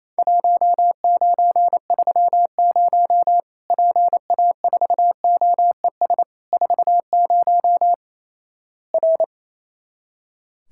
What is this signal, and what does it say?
Morse code, heard as a keyed tone, about 28 wpm, 700 Hz. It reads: 1930 PA4OES 40 R
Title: Worked All Europe DX Contest - QTC Exchange - 28wpm